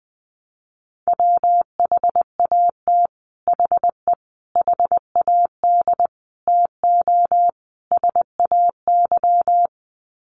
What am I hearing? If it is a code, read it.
Morse code, heard as a keyed tone, about 20 wpm, 700 Hz. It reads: WHAT HE HAD TO SAY
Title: Sets of 5 Words Encore2 20wpm